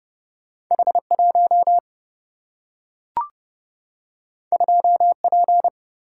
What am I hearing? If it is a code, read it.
Morse code, heard as a keyed tone, about 30 wpm, 700 Hz. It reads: H 1 2 P
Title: Single-Letter Single-Number 30wpm